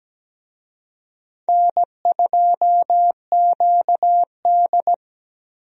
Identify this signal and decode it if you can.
Morse code, heard as a keyed tone, about 17 wpm, 700 Hz. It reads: N2QD
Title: USA Call Signs - 4 Characters - 17wpm